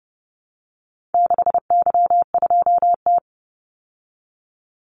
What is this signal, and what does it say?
Morse code, heard as a keyed tone, about 30 wpm, 700 Hz. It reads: 6Y2T
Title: Call Signs Repeated 3x - 30wpm